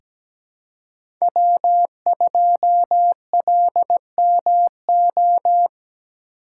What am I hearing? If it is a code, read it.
Morse code, heard as a keyed tone, about 17 wpm, 700 Hz. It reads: W2LMO
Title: USA Call Signs - 5 Characters - 3x Character Spacing - 17wpm